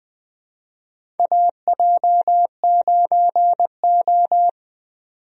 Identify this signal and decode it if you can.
Morse code, heard as a keyed tone, about 20 wpm, 700 Hz. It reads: AJ9O
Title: USA Call Signs - 4 Characters - 20wpm